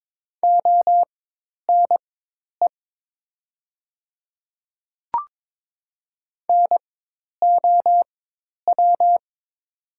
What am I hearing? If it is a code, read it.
Morse code, heard as a keyed tone, about 23 wpm, spaced as for 10 wpm, 700 Hz. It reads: ONE NOW
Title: 4x Character Spacing - Top 100 Words - 22wpm